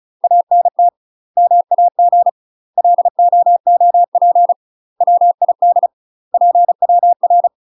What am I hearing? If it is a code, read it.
Morse code, heard as a keyed tone, about 34 wpm, 700 Hz. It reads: ANT MAG LOOP WID PWR
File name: UK MASTER QSOs TRAINING / 35WPM / UK MASTER QSOs 0005